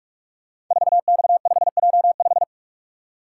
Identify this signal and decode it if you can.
Morse code, heard as a keyed tone, about 44 wpm, 700 Hz. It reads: 4X5J5
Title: Code Groups - 5 Letter:Numbers - 45wpm